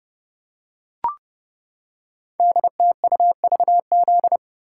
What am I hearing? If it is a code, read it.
Morse code, heard as a keyed tone, about 30 wpm, 700 Hz. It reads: DTUVZ
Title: Code Groups - 5 Letters - 30wpm